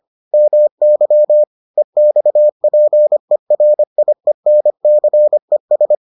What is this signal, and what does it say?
Morse code, heard as a keyed tone, about 25 wpm, 600 Hz. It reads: MY EXPERIENCES